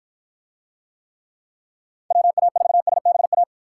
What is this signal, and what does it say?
Morse code, heard as a keyed tone, about 48 wpm, 700 Hz. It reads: WA4SBA